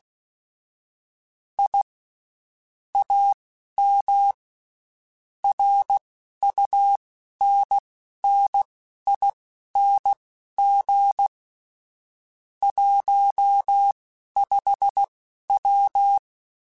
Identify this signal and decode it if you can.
Morse code, heard as a keyed tone, about 16 wpm, spaced as for 11 wpm, 800 Hz. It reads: I AM RUNNING 15W
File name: QSOs / QSOs-mp3s / WD0BGP 1@15